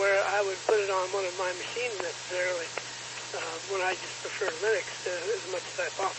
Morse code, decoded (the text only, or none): none